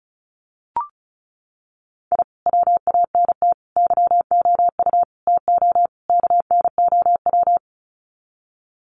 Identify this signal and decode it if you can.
Morse code, heard as a keyed tone, about 34 wpm, 700 Hz. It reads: I WANT YOU TO KNOW
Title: Sets of 5 Words Encore 35wpm